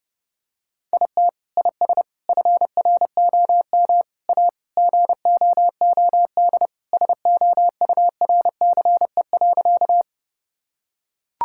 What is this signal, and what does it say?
Morse code, heard as a keyed tone, about 30 wpm, 700 Hz. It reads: IT IS FROM A GOOD SOURCE.
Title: Sentences from Top 1000 Words - Review Part 5 - 30wpm